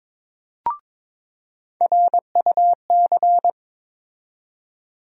Morse code, heard as a keyed tone, about 21 wpm, 700 Hz. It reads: RUC
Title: USA Call Sign Postfixes - 3 Letters 22wpm